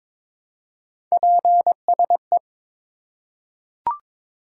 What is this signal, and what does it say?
Morse code, heard as a keyed tone, about 22 wpm, 700 Hz. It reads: PSE
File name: Top 90 QSO Elements - 4x Character Spacing - 22wpm